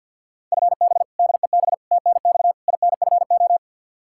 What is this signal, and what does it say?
Morse code, heard as a keyed tone, about 51 wpm, 700 Hz. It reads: FB DEB TNX INFO
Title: K1USN SST QSO Practice 50wpm